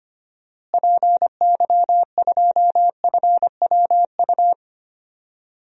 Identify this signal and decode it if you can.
Morse code, heard as a keyed tone, about 25 wpm, 700 Hz. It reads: PY2FWU
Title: DXCC-100 Call Signs - 6 Characters - Repeated 2x - 25wpm